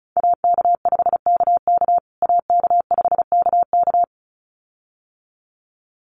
Morse code, heard as a keyed tone, about 34 wpm, 700 Hz. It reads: AK5KK AK5KK
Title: USA Call Signs Repeated 2x 35wpm